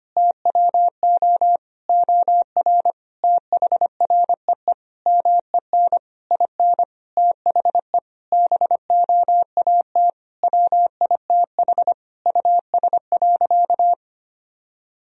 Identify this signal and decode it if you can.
Morse code, heard as a keyed tone, about 25 wpm, 700 Hz. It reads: TWO OR THREE MEN IN THE BOAT WITH US.
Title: Sentences from Top 1000 Words - Review Part 3 - 25wpm